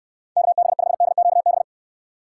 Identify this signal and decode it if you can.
Morse code, heard as a keyed tone, about 62 wpm, 700 Hz. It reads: KB5DQB